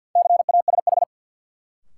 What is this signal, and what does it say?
Morse code, heard as a keyed tone, about 51 wpm, 700 Hz. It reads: CASH
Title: Sentences from Top 700 Words - New Word Drill - 50wpm